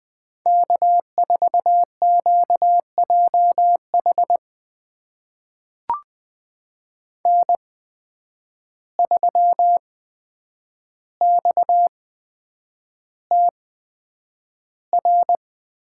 Morse code, heard as a keyed tone, about 20 wpm, spaced as for 9 wpm, 700 Hz. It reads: K4QJH N3XTR
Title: USA Call Signs - 5 Characters - 8x Character Spacing - 20wpm